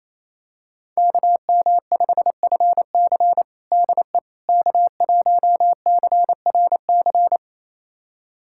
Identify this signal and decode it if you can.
Morse code, heard as a keyed tone, about 28 wpm, 690 Hz. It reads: KM5FC DE K1CRC
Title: USA Callsign DE Callsign - 5 Characters - 28wpm